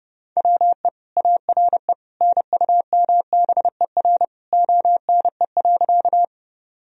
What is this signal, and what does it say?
Morse code, heard as a keyed tone, about 30 wpm, 700 Hz. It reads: WE ARE NUMBER ONE.
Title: Sentences from Top 1000 Words - Review Part 1 - 30wpm